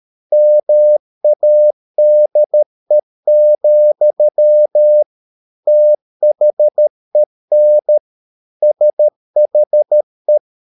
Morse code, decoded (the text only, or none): MADE, THEN SHE